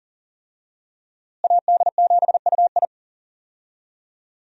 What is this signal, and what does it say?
Morse code, heard as a keyed tone, about 40 wpm, 700 Hz. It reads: AD7UI
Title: USA Call Signs - 5 Characters - Repeated 3x - 40wpm